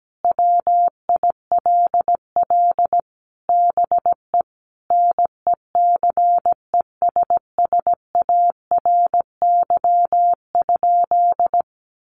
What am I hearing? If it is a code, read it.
Morse code, heard as a keyed tone, about 17 wpm, 700 Hz. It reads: WILL BE NECESSARY?
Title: Sentences from Top 800 Words 17wpm